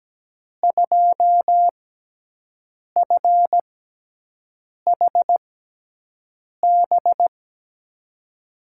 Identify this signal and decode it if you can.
Morse code, heard as a keyed tone, about 17 wpm, spaced as for 6 wpm, 700 Hz. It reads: 2FHB
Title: USA Call Signs - 5 Characters - 6x Character Spacing - 17wpm